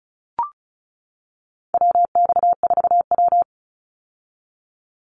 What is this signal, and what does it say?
Morse code, heard as a keyed tone, about 34 wpm, 700 Hz. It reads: WX4W
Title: Call Signs - 35wpm